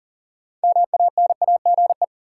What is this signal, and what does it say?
Morse code, heard as a keyed tone, about 40 wpm, 700 Hz. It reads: MANAGE
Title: Sentences from Top 1400 Words - New Word Drill - 40wpm